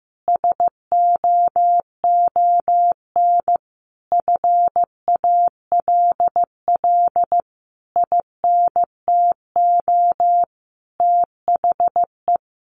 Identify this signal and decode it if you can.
Morse code, heard as a keyed tone, about 15 wpm, 700 Hz. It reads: SOON FALL INTO THE
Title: Sentences from Top 1300 Words 15wpm